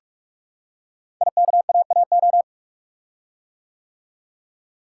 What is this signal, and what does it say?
Morse code, heard as a keyed tone, about 45 wpm, 700 Hz. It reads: IKAAO
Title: Code Groups - 5 Letters - 45wpm